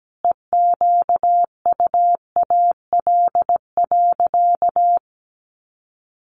Morse code, heard as a keyed tone, about 17 wpm, 700 Hz. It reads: EQUAL.
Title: Sentences from Top 1400 Words 17wpm